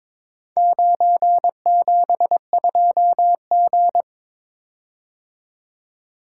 Morse code, heard as a keyed tone, about 22 wpm, 700 Hz. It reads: 972G
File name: practice / 22wpm / Four Letter-Number Combinations - 22wpm